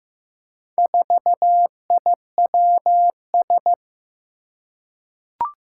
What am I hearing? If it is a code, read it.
Morse code, heard as a keyed tone, about 15 wpm, 700 Hz. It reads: 4IWS
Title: Four Challenging Letter-Number Combinations 15wpm